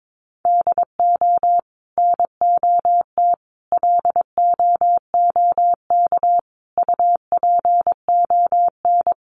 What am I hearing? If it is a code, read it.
Morse code, heard as a keyed tone, about 22 wpm, 700 Hz. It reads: DO NOT LOOK UPON
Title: Sentences from Top 1100 Words 22wpm